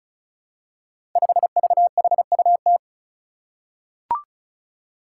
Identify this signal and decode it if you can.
Morse code, heard as a keyed tone, about 36 wpm, 700 Hz. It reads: 5VHUT
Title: Five Challenging Letter-Number Combinations 35wpm